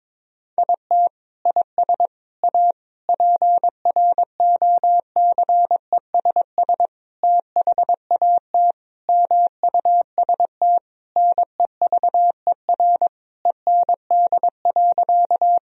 Morse code, heard as a keyed tone, about 22 wpm, 700 Hz. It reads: IT IS A PROCESS THAT MUST NEVER END.
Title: Sentences from Top 1000 Words - Review Part 3 - 22wpm